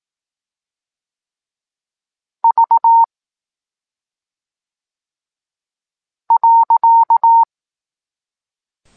Morse code, heard as a keyed tone, about 18 wpm, 920 Hz. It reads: V .
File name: Tekst-23 4